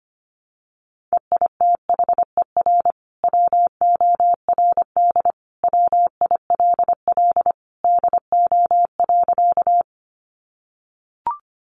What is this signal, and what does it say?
Morse code, heard as a keyed tone, about 25 wpm, 700 Hz. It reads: EITHER WORD WILL DO.
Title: Sentences from Top 600 Words 25wpm